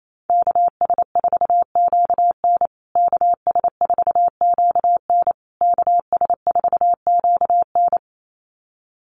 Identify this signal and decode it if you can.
Morse code, heard as a keyed tone, about 28 wpm, 700 Hz. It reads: KS4QN KS4QN KS4QN
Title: USA Call Signs Repeated 3x 28wpm